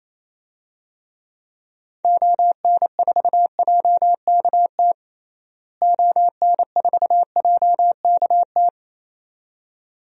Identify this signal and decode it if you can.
Morse code, heard as a keyed tone, about 28 wpm, 700 Hz. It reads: ON4JKT ON4JKT
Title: DXCC-100 Call Signs - 6 Characters - Repeated 2x - 28wpm